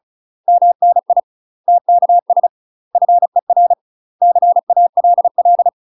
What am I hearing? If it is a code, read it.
Morse code, heard as a keyed tone, about 35 wpm, 700 Hz. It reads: MNI TKS FER CALL
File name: UK MASTER QSOs TRAINING / 35WPM / UK MASTER QSOs 0041